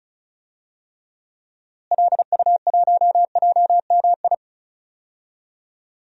Morse code, heard as a keyed tone, about 34 wpm, 700 Hz. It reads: LU1JMI
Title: DXCC-100 Call Signs - 6 Characters - Repeated 2x - 35wpm